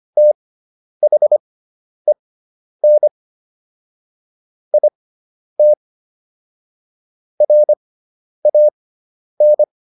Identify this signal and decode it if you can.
Morse code, heard as a keyed tone, about 25 wpm, spaced as for 10 wpm, 600 Hz. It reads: THEN IT RAN